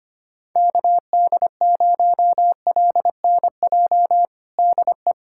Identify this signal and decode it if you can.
Morse code, heard as a keyed tone, about 25 wpm, 700 Hz. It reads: KD0LNJ DE
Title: USA Callsign DE Callsign 25wpm